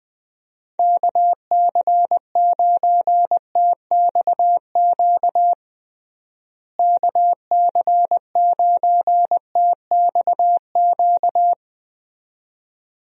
Morse code, heard as a keyed tone, about 21 wpm, 700 Hz. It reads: KC9TXQ KC9TXQ
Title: USA Call Signs - 6 Characters - Repeated 2x - 20wpm